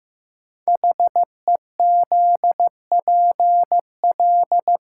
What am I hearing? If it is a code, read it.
Morse code, heard as a keyed tone, about 15 wpm, 700 Hz. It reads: HEZPL
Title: Code Groups - 5 Letters - 15wpm